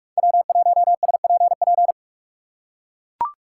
Morse code, heard as a keyed tone, about 44 wpm, 690 Hz. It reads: W1SPP
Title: USA Call Signs - 5 Characters - 6x Character Spacing - 45wpm